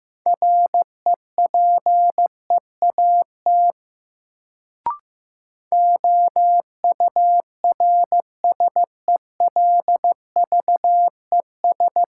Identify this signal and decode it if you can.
Morse code, heard as a keyed tone, about 15 wpm, 700 Hz. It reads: REPEAT OURSELVES
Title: Sentences from Top 1600 Words - New Word Drill Alt - 15wpm